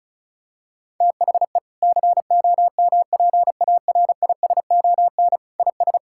Morse code, heard as a keyed tone, about 36 wpm, 700 Hz. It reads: THE COMPARISON IS
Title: Sentences from Top 1600 Words 35wpm